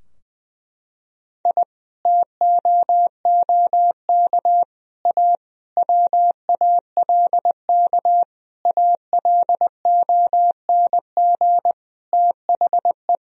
Morse code, heard as a keyed tone, about 20 wpm, 700 Hz. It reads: I TOOK A WALK ALONG THE
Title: Sentences from Top 1000 Words - Review Part 4 - 20wpm